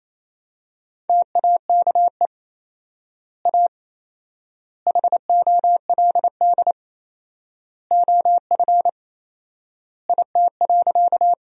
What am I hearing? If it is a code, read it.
Morse code, heard as a keyed tone, about 28 wpm, 700 Hz. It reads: TAKE A HOLD OF IT.
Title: 4x Word-Spacing - Sentences from Top 400 Words - 28wpm